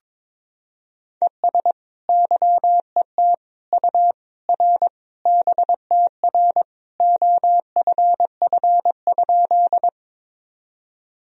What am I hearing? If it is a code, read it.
Morse code, heard as a keyed tone, about 22 wpm, 700 Hz. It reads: ES YET U R BTR OFF?
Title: Sentences from Top 500 Words and Abbreviations 22wpm